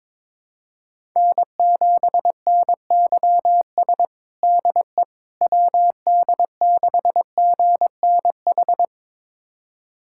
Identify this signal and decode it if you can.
Morse code, heard as a keyed tone, about 22 wpm, 700 Hz. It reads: N7NYS DE WD6GNH
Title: USA Callsign DE Callsign 22wpm